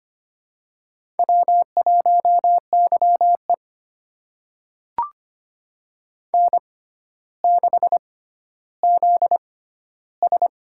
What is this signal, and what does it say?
Morse code, heard as a keyed tone, about 25 wpm, spaced as for 13 wpm, 700 Hz. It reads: W1YE N6ZS
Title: USA Call Signs - 4 Characters - 6x Character Spacing - 25wpm